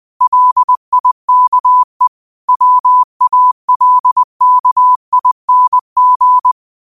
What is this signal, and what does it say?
Morse code, heard as a keyed tone, about 20 wpm, 1000 Hz. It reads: LIKE WALKING
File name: qotdfiles / 20wpm